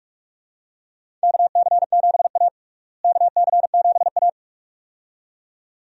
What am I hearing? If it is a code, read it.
Morse code, heard as a keyed tone, about 46 wpm, 700 Hz. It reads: KC7A KC7A
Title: USA Call Signs - 4 Characters - Repeated 2x - 45wpm